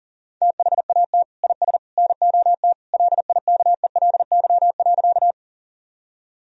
Morse code, heard as a keyed tone, about 40 wpm, 700 Hz. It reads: THAT IS NOT LIKELY.
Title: Sentences from Top 1000 Words - Review Part 4 - 40wpm